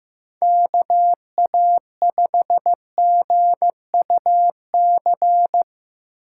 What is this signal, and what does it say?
Morse code, heard as a keyed tone, about 18 wpm, 700 Hz. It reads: KA5GUC
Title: Speed Racing - USA 2x3 Call Signs 15wpm